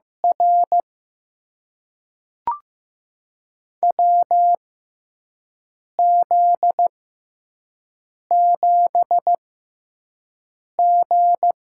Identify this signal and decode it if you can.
Morse code, heard as a keyed tone, about 15 wpm, spaced as for 5 wpm, 700 Hz. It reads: R WZ7G
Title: USA Call Signs - 6x Character Spacing - 15wpm